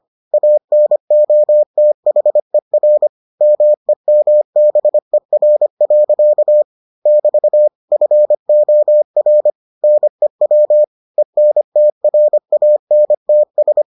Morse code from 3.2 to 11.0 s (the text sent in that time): MEMBER. <BT> FOR NEW